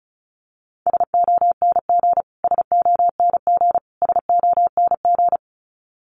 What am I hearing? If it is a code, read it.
Morse code, heard as a keyed tone, about 34 wpm, 700 Hz. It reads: SONG SONG SONG
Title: Top 500 Words Repeated 35wpm